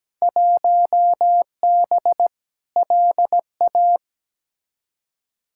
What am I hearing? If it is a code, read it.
Morse code, heard as a keyed tone, about 17 wpm, 700 Hz. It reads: 1B LA
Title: Field Day Exchanges - 17wpm